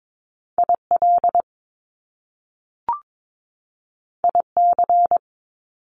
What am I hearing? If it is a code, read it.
Morse code, heard as a keyed tone, about 22 wpm, 700 Hz. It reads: IL IC
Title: Top Two Letter Combinations in Words 22wpm